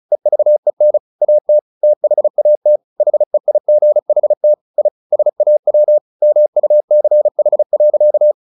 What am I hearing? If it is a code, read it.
Morse code, heard as a keyed tone, about 35 wpm, 600 Hz. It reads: EVEN AT THAT HEIGHT I SAW MUCH.
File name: the old man and the sea 2 35wpm